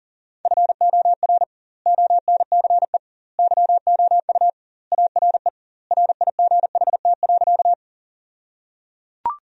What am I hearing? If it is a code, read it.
Morse code, heard as a keyed tone, about 37 wpm, 700 Hz. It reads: FOR ONCE YOU ARE RIGHT.
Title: 2x Word-Spacing - Sentences from Top 300 Words - 40wpm